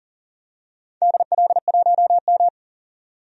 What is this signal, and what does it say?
Morse code, heard as a keyed tone, about 37 wpm, 700 Hz. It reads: DL1M
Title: DXCC-100 Call Signs - 4 Characters - 40wpm